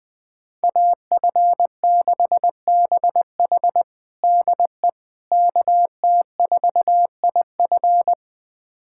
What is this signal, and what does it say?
Morse code, heard as a keyed tone, about 21 wpm, 700 Hz. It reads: AF6BH DE KT4IF
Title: USA Callsign DE Callsign - 5 Characters - 20wpm